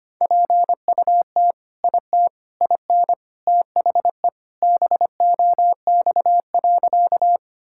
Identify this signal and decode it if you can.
Morse code, heard as a keyed tone, about 25 wpm, 700 Hz. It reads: PUT IT IN THE BOX.
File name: practice / 25wpm / Sentences with 2-Word Phrases - 25wpm